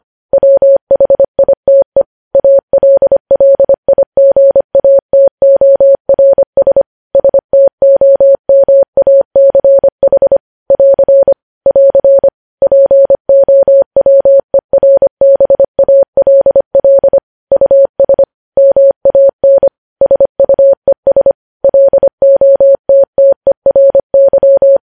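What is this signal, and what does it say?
Morse code, heard as a keyed tone, about 25 wpm, 550 Hz. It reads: WHITE ALLIGATORS STOMACH <AR> <AR> POWERBALL US MAN SUES LOTTERY
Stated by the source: News headlines sent in Morse code at 25.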